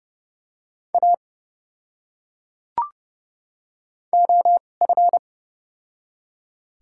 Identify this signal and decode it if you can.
Morse code, heard as a keyed tone, about 29 wpm, spaced as for 5 wpm, 700 Hz. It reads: AOF
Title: 2x Character Spacing - Top 200 Words - 30wpm